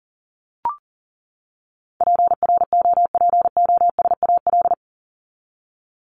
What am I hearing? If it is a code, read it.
Morse code, heard as a keyed tone, about 40 wpm, 700 Hz. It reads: PROPOSAL
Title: Top 1751-2000 Words 40wpm